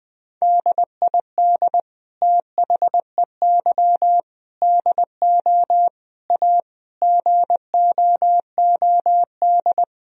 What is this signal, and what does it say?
Morse code, heard as a keyed tone, about 20 wpm, 700 Hz. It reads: DID THEY DO A GOOD
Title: Sentences from Top 1000 Words - Review Part 3 - 20wpm